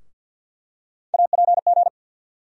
Morse code, heard as a keyed tone, about 53 wpm, 700 Hz. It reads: APG
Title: USA Call Sign Postfixes - 3 Letters 50wpm